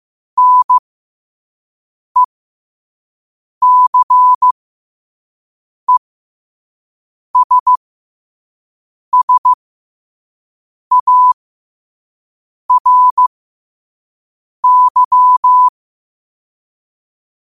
Morse code, heard as a keyed tone, about 15 wpm, spaced as for 5 wpm, 1000 Hz. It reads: NECESSARY
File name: qotdfiles / 5wpm